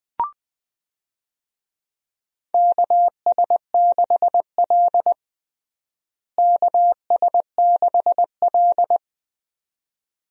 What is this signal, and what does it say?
Morse code, heard as a keyed tone, about 20 wpm, 700 Hz. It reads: KS6L KS6L
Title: USA Call Signs - 4 Characters - Repeated 2x - 20wpm